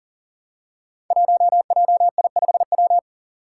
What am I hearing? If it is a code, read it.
Morse code, heard as a keyed tone, about 40 wpm, 700 Hz. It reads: 1JI5W
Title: Five Challenging Letter-Number Combinations 40wpm